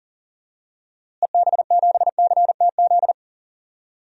Morse code, heard as a keyed tone, about 40 wpm, 700 Hz. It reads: EB7CTZ
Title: DXCC-100 Call Signs - 6 Characters - Repeated 3x - 40wpm